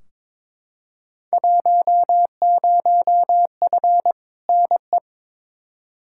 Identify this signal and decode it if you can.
Morse code, heard as a keyed tone, about 22 wpm, 700 Hz. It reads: 10F NE
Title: Field Day Exchanges Encore - 22wpm